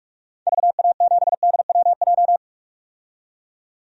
Morse code, heard as a keyed tone, about 46 wpm, 700 Hz. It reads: VA7DWJ
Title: DXCC-100 Call Signs - 6 Characters - 45wpm